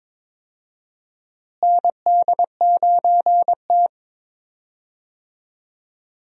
Morse code, heard as a keyed tone, about 22 wpm, 700 Hz. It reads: ND9T
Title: USA Call Signs - 4 Characters - Repeated 2x - 22wpm